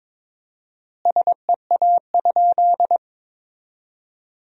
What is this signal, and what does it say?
Morse code, heard as a keyed tone, about 22 wpm, 700 Hz. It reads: SEA?
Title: 4x Word-Spacing - Sentences from Top 500 Words - 22wpm